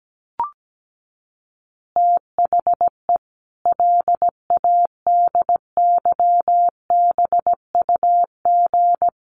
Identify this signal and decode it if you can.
Morse code, heard as a keyed tone, about 17 wpm, 700 Hz. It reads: THE LADYBUG
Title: Additional Easy Phrases 17wpm